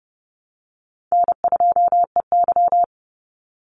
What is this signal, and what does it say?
Morse code, heard as a keyed tone, about 30 wpm, 700 Hz. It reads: N2EY
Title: Easy Call Signs - 30wpm